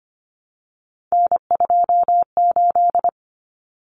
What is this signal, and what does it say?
Morse code, heard as a keyed tone, about 25 wpm, 700 Hz. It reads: N28
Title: Common Cut Numbers - 3-Digit 25wpm